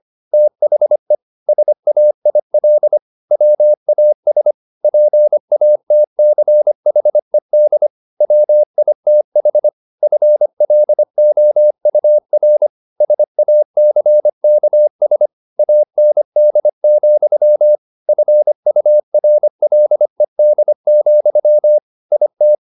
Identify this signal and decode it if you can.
Morse code, heard as a keyed tone, about 25 wpm, 600 Hz. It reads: THE SAIL WAS PATCHED WITH FLOUR SACKS AND, FURLED, IT
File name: the old man and the sea part 1 25wpm